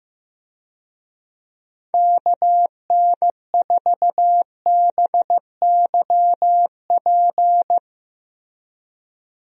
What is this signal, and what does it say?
Morse code, heard as a keyed tone, about 16 wpm, 700 Hz. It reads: KN4BYP
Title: USA Call Signs - 6 Characters - Repeated 2x - 15wpm